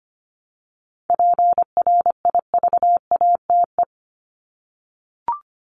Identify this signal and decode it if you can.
Morse code, heard as a keyed tone, about 25 wpm, 700 Hz. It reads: PRIVATE
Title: Top 751-1000 Words 25wpm